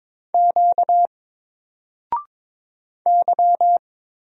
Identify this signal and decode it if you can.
Morse code, heard as a keyed tone, about 22 wpm, 700 Hz. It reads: Q Y
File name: Single-Letter-Number-22wpm